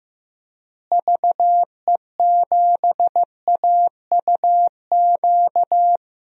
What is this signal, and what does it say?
Morse code, heard as a keyed tone, about 15 wpm, 700 Hz. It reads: VE7AUQ
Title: DXCC-100 Call Signs - 6 Characters - Repeated 3x - 15wpm